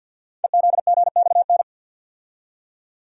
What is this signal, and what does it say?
Morse code, heard as a keyed tone, about 50 wpm, 700 Hz. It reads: EZGXN
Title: Code Groups - 5 Letter:Numbers - 50wpm